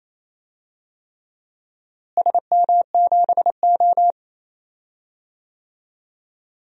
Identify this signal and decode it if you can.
Morse code, heard as a keyed tone, about 28 wpm, 700 Hz. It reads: SM7O
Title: DXCC-100 Call Signs - 4 Characters - Repeated 3x - 28wpm